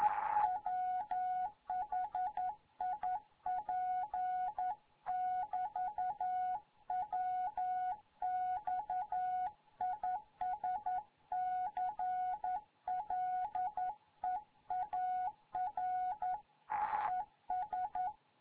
Morse code, heard as a keyed tone, about 11 wpm, 740 Hz. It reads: WHIP <BT> WX IS CLEAR ES
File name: Morze-HA2MN